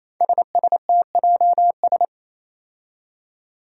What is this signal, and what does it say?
Morse code, heard as a keyed tone, about 28 wpm, 700 Hz. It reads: SSTJS
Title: Five Challenging Letter-Number Combinations 28wpm